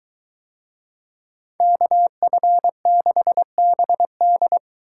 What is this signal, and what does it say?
Morse code, heard as a keyed tone, about 23 wpm, 700 Hz. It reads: KF6BD